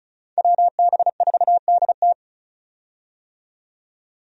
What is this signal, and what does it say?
Morse code, heard as a keyed tone, about 36 wpm, 700 Hz. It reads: WB4DT
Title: USA Call Signs - 5 Characters - 35wpm